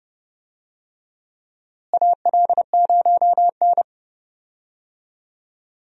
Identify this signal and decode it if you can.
Morse code, heard as a keyed tone, about 32 wpm, 700 Hz. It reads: AL0N
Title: USA Call Signs - 4 Characters - Repeated 3x - 30wpm